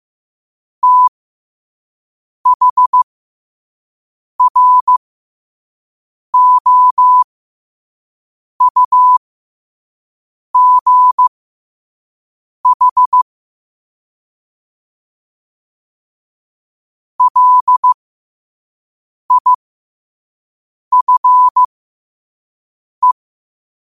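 Morse code, heard as a keyed tone, about 15 wpm, spaced as for 5 wpm, 1000 Hz. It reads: THROUGH LIFE
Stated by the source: Morse code quote of the day sent at 5 wpm.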